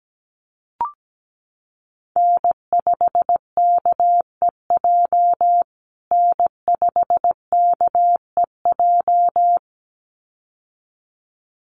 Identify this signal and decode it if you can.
Morse code, heard as a keyed tone, about 17 wpm, 700 Hz. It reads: N5KEJ N5KEJ
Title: USA Call Signs Repeated 2x 17wpm